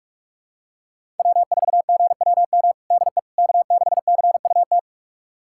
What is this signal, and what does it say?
Morse code, heard as a keyed tone, about 46 wpm, 700 Hz. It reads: W4GWM DE K6CUT
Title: USA Callsign DE Callsign - 5 Characters - 45wpm